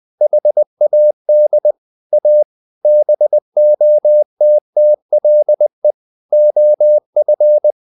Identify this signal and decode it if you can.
Morse code, heard as a keyed tone, about 20 wpm, 600 Hz. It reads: HAD A BOTTLE OF
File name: the old man and the sea / the old man and the sea part 1 20wpm